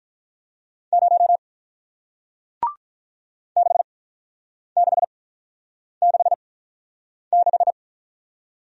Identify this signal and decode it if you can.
Morse code, heard as a keyed tone, about 46 wpm, spaced as for 9 wpm, 700 Hz. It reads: 0 6666